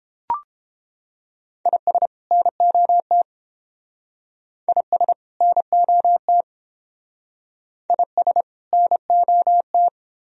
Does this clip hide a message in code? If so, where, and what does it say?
Yes: Morse code from 1.7 to 9.9 s, heard as a keyed tone, about 28 wpm, 700 Hz. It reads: IS NOT IS NOT IS NOT